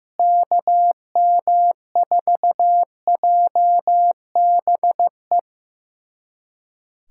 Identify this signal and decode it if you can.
Morse code, heard as a keyed tone, about 15 wpm, 700 Hz. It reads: KM4JBE
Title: USA Callsign DE Callsign - 6 Characters - 15wpm